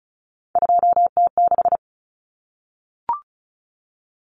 Morse code, heard as a keyed tone, about 34 wpm, 700 Hz. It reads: 2T6
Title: Common Cut Numbers - 3-Digit 35wpm